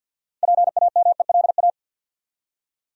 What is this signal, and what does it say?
Morse code, heard as a keyed tone, about 53 wpm, 700 Hz. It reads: PAMELA
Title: USA Common Names - 50wpm